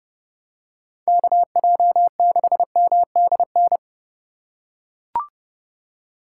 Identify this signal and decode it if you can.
Morse code, heard as a keyed tone, about 30 wpm, 700 Hz. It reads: KJ6MDN